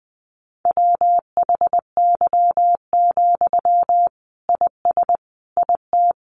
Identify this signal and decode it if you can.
Morse code, heard as a keyed tone, about 20 wpm, 700 Hz. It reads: WHY, IS IT
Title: Sentences from Top 300 Words 20wpm